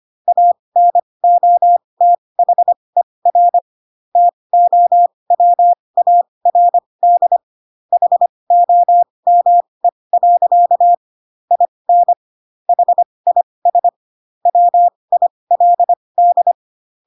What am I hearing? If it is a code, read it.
Morse code, heard as a keyed tone, about 25 wpm, spaced as for 20 wpm, 700 Hz. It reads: ANOTHER TOWARD HOME. IN HIS WILD
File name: Pinocchio / Pinocchio ch4 15 25 ik1hge